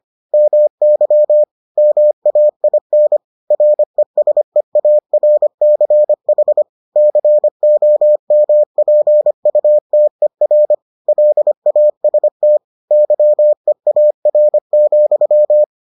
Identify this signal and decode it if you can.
Morse code, heard as a keyed tone, about 25 wpm, 600 Hz. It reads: MY MAIN RESEARCH COMPUTER LAST YEAR,